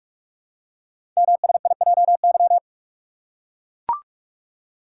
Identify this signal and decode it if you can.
Morse code, heard as a keyed tone, about 44 wpm, 700 Hz. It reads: MSIJY
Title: Code Groups - 5 Letters - 45wpm